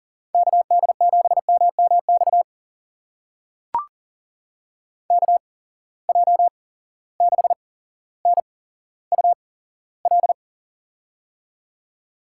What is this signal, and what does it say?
Morse code, heard as a keyed tone, about 40 wpm, spaced as for 17 wpm, 700 Hz. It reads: KD7MMX KJ6NUL
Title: USA Call Signs - 6 Characters - 8x Character Spacing - 40wpm